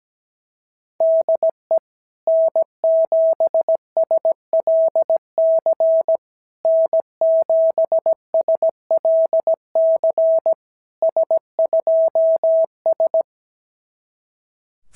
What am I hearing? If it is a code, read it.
Morse code, heard as a keyed tone, about 17 wpm, 650 Hz. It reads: DE N7SLC N7SLC S2S
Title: SOTA QSO Practice 17wpm